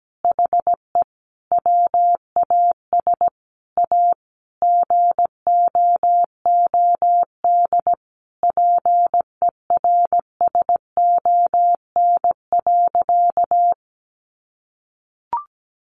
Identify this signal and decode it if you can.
Morse code, heard as a keyed tone, about 17 wpm, 700 Hz. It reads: HE WAS A GOOD PERSON.
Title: Sentences from Top 1300 Words 17wpm